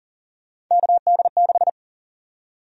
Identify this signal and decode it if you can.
Morse code, heard as a keyed tone, about 40 wpm, 700 Hz. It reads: KD6
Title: USA Call Sign Prefixes - Repeated 2x - 40wpm